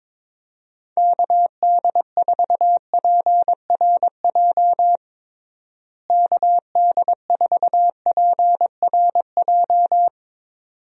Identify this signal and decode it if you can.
Morse code, heard as a keyed tone, about 22 wpm, 700 Hz. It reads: KD4PRJ KD4PRJ
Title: USA Call Signs - 6 Characters - Repeated 2x - 22wpm